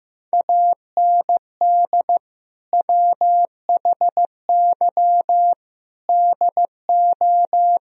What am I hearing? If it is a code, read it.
Morse code, heard as a keyed tone, about 15 wpm, 700 Hz. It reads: AND WHY DO
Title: Sentences from Top 1000 Words - Review Part 2 - 15wpm